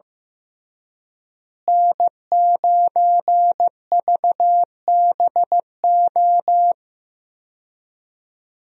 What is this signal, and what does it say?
Morse code, heard as a keyed tone, about 15 wpm, 700 Hz. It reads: N9VBO
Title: USA Call Signs - 5 Characters - Repeated 3x - 15wpm